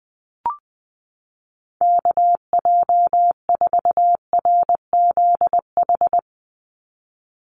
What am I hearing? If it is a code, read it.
Morse code, heard as a keyed tone, about 20 wpm, 700 Hz. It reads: KJ4RZH
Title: USA Call Signs 20wpm